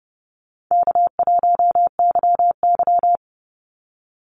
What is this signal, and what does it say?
Morse code, heard as a keyed tone, about 30 wpm, 700 Hz. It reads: K1YY
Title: Easy Call Signs Repeated 3x - 30wpm